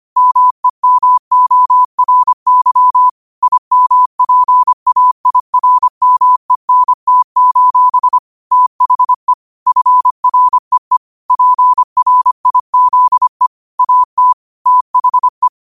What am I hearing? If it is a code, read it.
Morse code, heard as a keyed tone, about 25 wpm, 1000 Hz. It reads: MEMORY IMPAIRMENT: THE FREE PRIZE AT THE
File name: qotdfiles / 25wpm